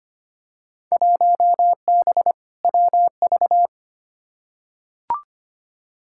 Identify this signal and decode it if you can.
Morse code, heard as a keyed tone, about 25 wpm, 700 Hz. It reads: 1B WV
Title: Field Day Exchanges Encore - 25wpm